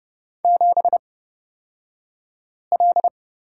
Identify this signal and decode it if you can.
Morse code, heard as a keyed tone, about 30 wpm, spaced as for 5 wpm, 700 Hz. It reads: 7L
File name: Lesson 36d - BT - Two Characters 30wpm